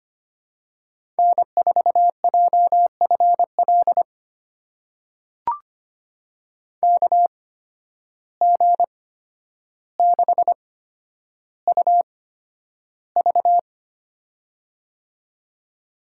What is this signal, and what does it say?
Morse code, heard as a keyed tone, about 25 wpm, spaced as for 11 wpm, 700 Hz. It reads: N4JFL KG6UV
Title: USA Call Signs - 5 Characters - 8x Character Spacing - 25wpm